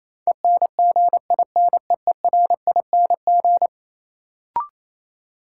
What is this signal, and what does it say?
Morse code, heard as a keyed tone, about 28 wpm, 700 Hz. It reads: ENGINEERING
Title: Sentences from Top 1500 Words - New Word Drill Alt - 28wpm